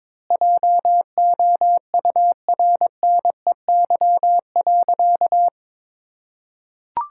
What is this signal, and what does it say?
Morse code, heard as a keyed tone, about 21 wpm, 700 Hz. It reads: JOURNEY.
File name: Sentences with 9-Word Phrases - 22wpm